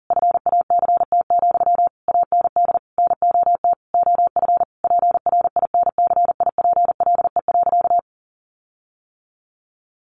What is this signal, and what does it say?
Morse code, heard as a keyed tone, about 37 wpm, 700 Hz. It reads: FACT, AND NOT OF PRINCIPLE.
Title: Sentences from Top 1400 Words 40wpm